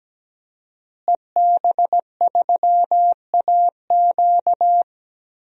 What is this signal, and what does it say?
Morse code, heard as a keyed tone, about 17 wpm, 700 Hz. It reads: EB3AQ
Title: DXCC-100 Call Signs - 5 Characters - Repeated 2x - 17wpm